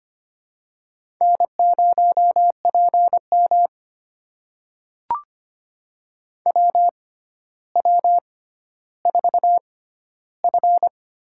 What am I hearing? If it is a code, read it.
Morse code, heard as a keyed tone, about 25 wpm, spaced as for 13 wpm, 700 Hz. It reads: N0PM WW4F
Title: USA Call Signs - 4 Characters - 6x Character Spacing - 25wpm